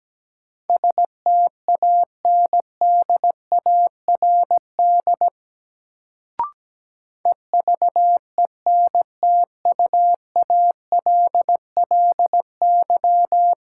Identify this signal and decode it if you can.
Morse code, heard as a keyed tone, about 17 wpm, 700 Hz. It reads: STANDARD EVENTUALLY
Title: Sentences from Top 1000 Words - Review Part 6 - New Word Drill - 17wpm